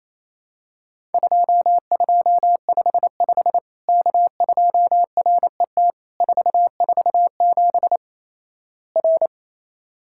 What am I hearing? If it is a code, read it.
Morse code, heard as a keyed tone, about 28 wpm, 700 Hz. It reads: 2255 K2RET 447 R
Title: Worked All Europe DX Contest - QTC Exchange - 28wpm